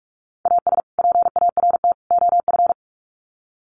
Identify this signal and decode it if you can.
Morse code, heard as a keyed tone, about 44 wpm, 700 Hz. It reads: AS PART OF
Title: Sets of 3 Words 45wpm